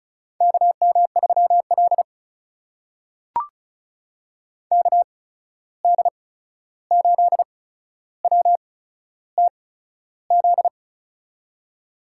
Morse code, heard as a keyed tone, about 36 wpm, spaced as for 13 wpm, 700 Hz. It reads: KM3L KD8WTZ
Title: USA Call Signs - 8x Character Spacing - 35wpm